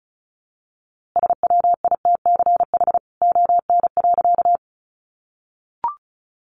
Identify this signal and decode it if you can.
Morse code, heard as a keyed tone, about 34 wpm, 700 Hz. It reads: SWITCH ON.
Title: A Sound Of Thunder 35wpm